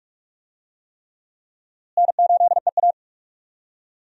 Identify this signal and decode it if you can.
Morse code, heard as a keyed tone, about 44 wpm, 700 Hz. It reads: N8EA
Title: USA Call Signs - 4 Characters - Repeated 3x - 45wpm